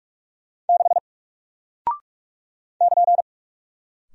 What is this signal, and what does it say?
Morse code, heard as a keyed tone, about 45 wpm, 700 Hz. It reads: 6 <KN>
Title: Single Character (Letter:Number:Punctuation:Prosign) Alt 45wpm